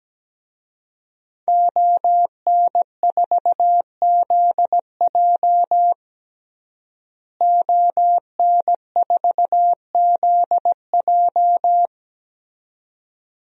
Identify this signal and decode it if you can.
Morse code, heard as a keyed tone, about 17 wpm, 700 Hz. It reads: ON4ZJ ON4ZJ
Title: DXCC-100 Call Signs - 5 Characters - Repeated 2x - 17wpm